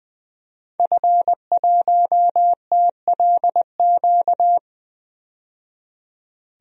Morse code, heard as a keyed tone, about 20 wpm, 700 Hz. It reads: F1TLQ
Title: DXCC-100 Call Signs - 5 Characters - 20wpm